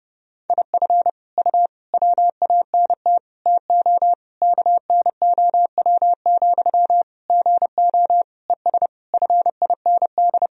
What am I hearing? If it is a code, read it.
Morse code, heard as a keyed tone, about 32 wpm, 700 Hz. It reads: IF U WANT TO KNOW, GO ES FIND
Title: Sentences from Top 200 Words and Abbreviations 30wpm